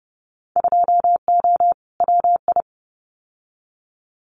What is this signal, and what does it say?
Morse code, heard as a keyed tone, about 30 wpm, 700 Hz. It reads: 2O WI
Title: Winter Field Day Exchanges 30wpm